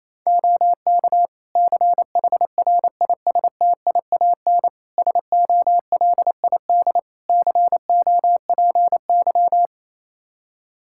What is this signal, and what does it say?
Morse code, heard as a keyed tone, about 28 wpm, 700 Hz. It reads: OK CHRISTIAN SOLID COPY
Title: Practice QSO - Exchange 3 - More Info - 28wpm